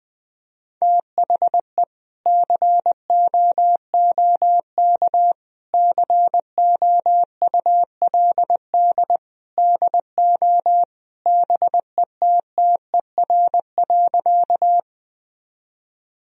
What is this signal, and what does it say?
Morse code, heard as a keyed tone, about 20 wpm, 700 Hz. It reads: THE COOK COULD DO BETTER.
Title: Sentences from Top 1000 Words - Review Part 4 - 20wpm